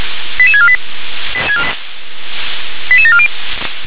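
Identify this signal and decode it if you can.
Five-tone selective call (ZVEI), heard as quick runs of five tones, tones of about 70 ms. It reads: 88648 88740
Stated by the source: ZVEI five-tone selective call received on SDR.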